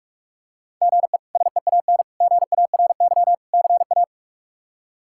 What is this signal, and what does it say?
Morse code, heard as a keyed tone, about 46 wpm, 700 Hz. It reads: GE SEAN GARY CA
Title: K1USN SST QSO Practice - 45wpm